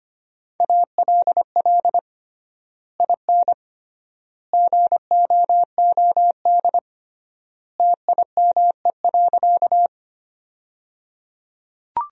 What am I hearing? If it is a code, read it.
Morse code, heard as a keyed tone, about 25 wpm, 700 Hz. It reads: ALL IN GOOD TIME.
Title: 3x Word-Spacing - Sentences from Top 200 Words - 25wpm